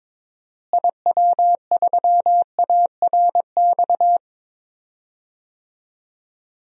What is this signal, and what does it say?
Morse code, heard as a keyed tone, about 22 wpm, 700 Hz. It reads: IW3ARX
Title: DXCC-100 Call Signs - 6 Characters - Repeated 3x - 22wpm